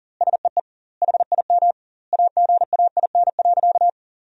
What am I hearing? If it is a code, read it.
Morse code, heard as a keyed tone, about 40 wpm, 700 Hz. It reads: SEE HIM AGAIN.
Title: 2x Word-Spacing - Sentences from Top 300 Words - 40wpm